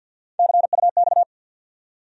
Morse code, heard as a keyed tone, about 48 wpm, 700 Hz. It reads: CUX
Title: USA Call Sign Postfixes - 3 Letters 50wpm